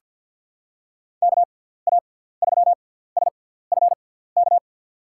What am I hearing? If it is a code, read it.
Morse code, heard as a keyed tone, about 47 wpm, spaced as for 17 wpm, 700 Hz. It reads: KA3SFK
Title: USA Call Signs - 6 Characters - 6x Character Spacing - 50wpm